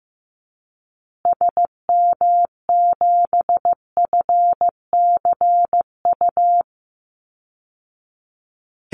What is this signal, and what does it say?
Morse code, heard as a keyed tone, about 15 wpm, 700 Hz. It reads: SM7FCU
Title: Call Signs Repeated 2x - 15wpm